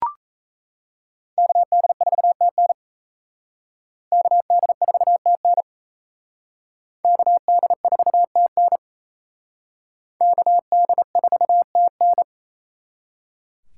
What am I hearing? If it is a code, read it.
Morse code, heard as a keyed tone, about 34 wpm, 700 Hz. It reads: KD4TN KD4TN KD4TN KD4TN
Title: Speed Racing - USA Call Signs 28wpm